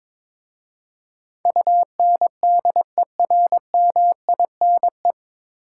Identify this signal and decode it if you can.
Morse code, heard as a keyed tone, about 22 wpm, 700 Hz. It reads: UNDERMINE
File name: 9 Letter Words - First 3 Letters Given - 22wpm